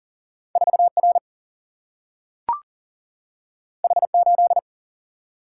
Morse code, heard as a keyed tone, about 39 wpm, 700 Hz. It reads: 4R H8
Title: Single-Letter Single-Number 40wpm